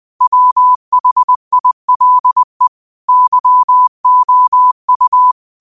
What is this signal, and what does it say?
Morse code, heard as a keyed tone, about 20 wpm, 1000 Hz. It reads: WHILE YOU